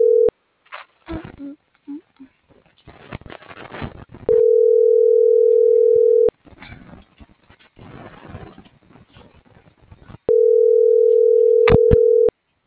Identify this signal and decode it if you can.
Ringback tone (North America), 440 + 480 Hz.